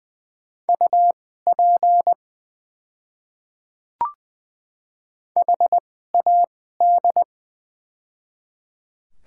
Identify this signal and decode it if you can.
Morse code, heard as a keyed tone, about 20 wpm, spaced as for 14 wpm, 690 Hz. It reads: UP HAD
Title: 2x Character Spacing - Top 100 Words - 20wpm